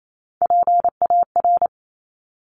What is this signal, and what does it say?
Morse code, heard as a keyed tone, about 28 wpm, 700 Hz. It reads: PAR
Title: Top Three Letter Combinations in Words 28wpm